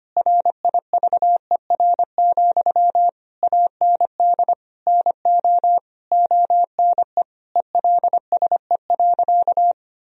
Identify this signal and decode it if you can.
Morse code, heard as a keyed tone, about 25 wpm, 700 Hz. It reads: RIVER, AND NO ONE ELSE.
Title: Sentences from Top 1000 Words - Review Part 1 - 25wpm